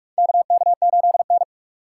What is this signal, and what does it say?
Morse code, heard as a keyed tone, about 47 wpm, 700 Hz. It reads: KK8N
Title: USA Call Signs - 4 Characters - 2x Character Spacing - 45wpm